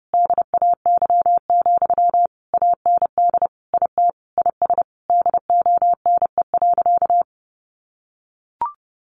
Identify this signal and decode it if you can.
Morse code, heard as a keyed tone, about 30 wpm, 700 Hz. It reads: DAY, AND IT IS DONE.
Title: Sentences from Top 1500 Words 30wpm